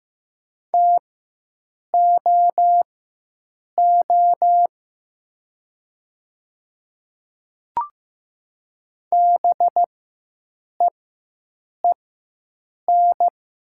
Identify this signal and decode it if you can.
Morse code, heard as a keyed tone, about 15 wpm, spaced as for 7 wpm, 700 Hz. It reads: TOO BEEN
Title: 4x Character Spacing - Top 200 Words - 15wpm